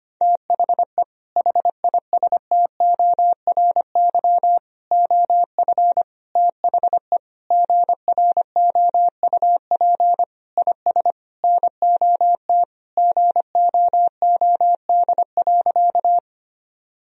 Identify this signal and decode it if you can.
Morse code, heard as a keyed tone, about 25 wpm, 700 Hz. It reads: THE HISTORY OF THE GROUP IS NOT GOOD.
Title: Sentences from Top 1000 Words - Review Part 1 - 25wpm